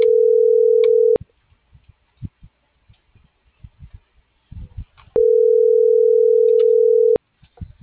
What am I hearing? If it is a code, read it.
Ringback tone (North America), 440 + 480 Hz.